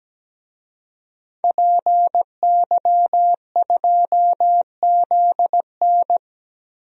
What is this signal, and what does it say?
Morse code, heard as a keyed tone, about 17 wpm, 700 Hz. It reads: PY2ZN
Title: DXCC-100 Call Signs - 5 Characters - Repeated 3x - 17wpm